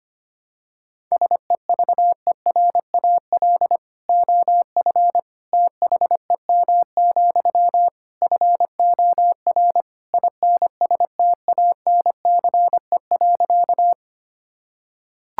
Morse code, heard as a keyed tone, about 25 wpm, 700 Hz. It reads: SEVERAL OF THEM, FOR INSTANCE.
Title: Sentences from Top 1000 Words - Review Part 5 - 25wpm